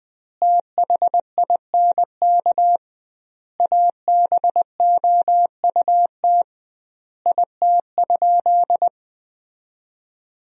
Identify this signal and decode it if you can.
Morse code, heard as a keyed tone, about 20 wpm, 700 Hz. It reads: THINK ABOUT IT?
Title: 2x Word-Spacing - Sentences from Top 200 Words - 20wpm